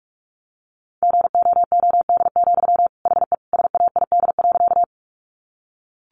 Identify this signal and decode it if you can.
Morse code, heard as a keyed tone, about 44 wpm, 700 Hz. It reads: GOOD, HE SAID.
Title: Sentences from Top 200 Words 45wpm